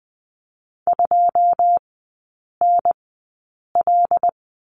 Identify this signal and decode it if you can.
Morse code, heard as a keyed tone, about 20 wpm, spaced as for 8 wpm, 700 Hz. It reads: 2NL
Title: 081 Lesson 14e - Number 2 - Three Characters 20wpm